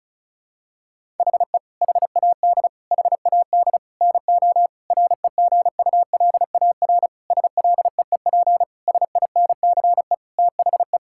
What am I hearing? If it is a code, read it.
Morse code, heard as a keyed tone, about 36 wpm, 700 Hz. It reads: HE HAD HAD NO REGULAR SLEEP SINCE THE
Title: Sentences from Top 1000 Words - Review Part 6 - 35wpm